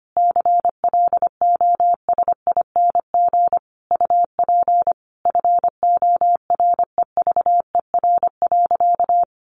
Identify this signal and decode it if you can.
Morse code, heard as a keyed tone, about 25 wpm, 700 Hz. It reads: CLOSING UP FOREVER.
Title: A Sound Of Thunder 25wpm